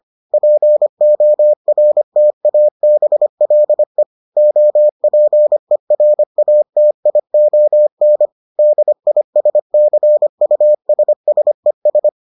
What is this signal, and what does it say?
Morse code, heard as a keyed tone, about 25 wpm, 600 Hz. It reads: PORTABLE OPERATION DISCUSSES